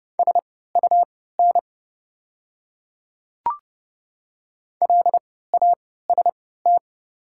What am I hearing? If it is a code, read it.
Morse code, heard as a keyed tone, about 30 wpm, spaced as for 17 wpm, 700 Hz. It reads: SUN LAST
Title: 3x Character Spacing - Top 300 Words - 30wpm